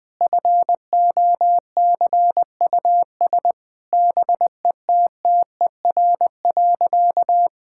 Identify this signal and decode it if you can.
Morse code, heard as a keyed tone, about 20 wpm, 700 Hz. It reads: FOCUS BETTER.
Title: Difficult Sentences with Hints - 20wpm